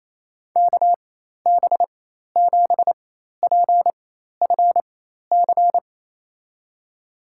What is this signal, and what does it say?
Morse code, heard as a keyed tone, about 28 wpm, spaced as for 13 wpm, 700 Hz. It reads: KB7PFC
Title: USA Call Signs - 4x Character Spacing - 28wpm